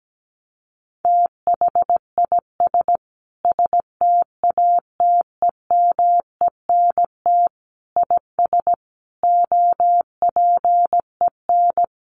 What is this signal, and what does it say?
Morse code, heard as a keyed tone, about 17 wpm, 700 Hz. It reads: THIS STATEMENT IS OPEN
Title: Sentences from Top 1300 Words 17wpm